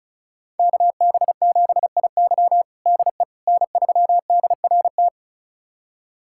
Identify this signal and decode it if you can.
Morse code, heard as a keyed tone, about 36 wpm, 700 Hz. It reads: KB7IY DE N3DRT
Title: USA Callsign DE Callsign - 5 Characters - 35wpm